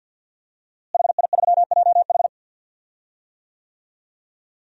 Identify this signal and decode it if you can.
Morse code, heard as a keyed tone, about 52 wpm, 700 Hz. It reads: HI3JH
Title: DXCC-100 Call Signs - 5 Characters - Repeated 2x - 50wpm